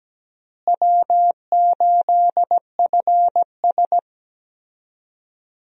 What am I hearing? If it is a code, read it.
Morse code, heard as a keyed tone, about 17 wpm, 700 Hz. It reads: W8FS
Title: USA Call Signs - 4 Characters - Repeated 3x - 17wpm